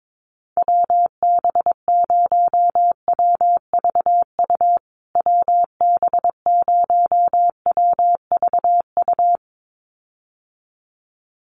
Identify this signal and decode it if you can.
Morse code, heard as a keyed tone, about 22 wpm, 700 Hz. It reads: WB0WVU WB0WVU
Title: USA Call Signs Repeated 2x 22wpm